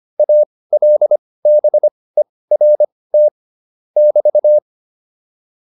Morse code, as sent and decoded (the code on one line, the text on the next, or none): .- .-.. -... .-. - / -...-
ALBERT <BT>